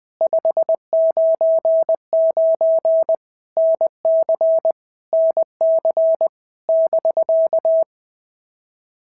Morse code, heard as a keyed tone, about 20 wpm, 650 Hz. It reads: 599 NC NC <BK>
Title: POTA QSO Practice - 20wpm